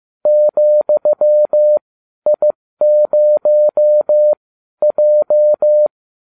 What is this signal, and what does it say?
Morse code, heard as a keyed tone, about 15 wpm, spaced as for 12 wpm, 600 Hz. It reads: ,I0J